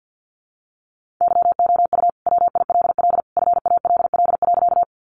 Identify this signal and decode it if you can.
Morse code, heard as a keyed tone, about 50 wpm, 700 Hz. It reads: YOU WILL FALL.
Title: Sentences from Top 500 Words 50wpm